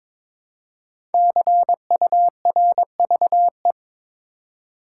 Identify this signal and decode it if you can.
Morse code, heard as a keyed tone, about 23 wpm, 700 Hz. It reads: CURVE
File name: 5 Letter Words - First 3 Letters Given - 22wpm